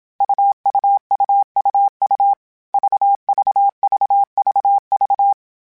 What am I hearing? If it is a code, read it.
Morse code, heard as a keyed tone, about 27 wpm, 800 Hz. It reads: UUUUU VVVVV
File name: Difficult Combos / Difficult Combos-mp3s / AUV AUV4 @25